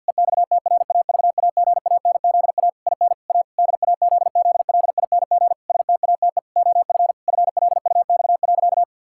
Morse code, heard as a keyed tone, about 50 wpm, 700 Hz. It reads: EXTRAVAGANZA IN A DAZZLING STATE OF FLUX.
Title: Pangram Challenge v2 - 50wpm